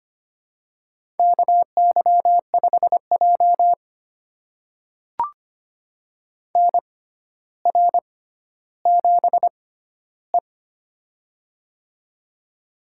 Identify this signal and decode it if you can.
Morse code, heard as a keyed tone, about 25 wpm, spaced as for 13 wpm, 700 Hz. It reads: KY5J NR7E
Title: USA Call Signs - 4 Characters - 6x Character Spacing - 25wpm